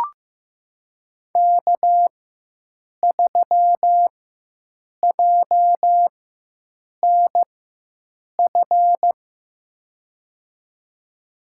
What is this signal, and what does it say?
Morse code, heard as a keyed tone, about 15 wpm, spaced as for 7 wpm, 700 Hz. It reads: K3JNF
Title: USA Call Signs - 5 Characters - 4x Character Spacing - 15wpm